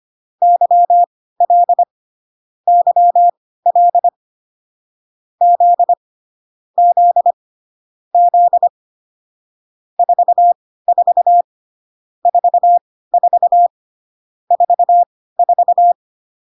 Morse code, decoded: YL YL Z Z Z 44 44 44

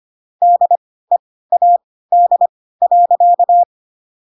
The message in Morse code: DEAD.

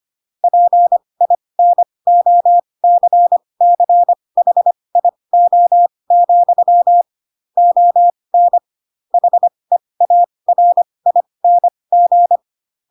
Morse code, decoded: PINOCCHIO, ON HEARING

